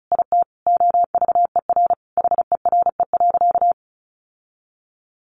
Morse code, heard as a keyed tone, about 34 wpm, 700 Hz. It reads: IT OVER HERE.
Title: Sentences from Top 1500 Words 35wpm